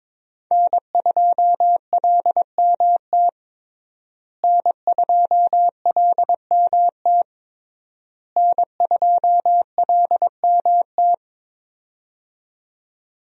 Morse code, heard as a keyed tone, about 22 wpm, 700 Hz. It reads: N2LMT N2LMT N2LMT
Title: USA Call Signs - 5 Characters - Repeated 3x - 22wpm